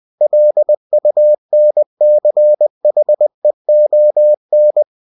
Morse code, heard as a keyed tone, about 20 wpm, 600 Hz. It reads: LUNCHEON
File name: cw-books / winnie-the-pooh / winnie-20wpm-page-0025